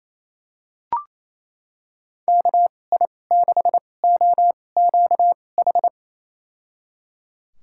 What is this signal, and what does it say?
Morse code, heard as a keyed tone, about 28 wpm, spaced as for 20 wpm, 700 Hz. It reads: KI6OQH
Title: USA Call Signs - 2x Character Spacing - 28wpm